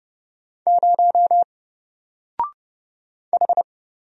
Morse code, heard as a keyed tone, about 29 wpm, 700 Hz. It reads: 0 H
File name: Single-Letter-Number-30wpm